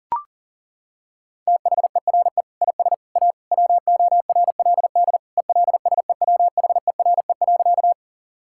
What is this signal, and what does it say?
Morse code, heard as a keyed tone, about 37 wpm, 700 Hz. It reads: THERE IS A WORLD ELSEWHERE.
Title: Sentences from Top 1600 Words 40wpm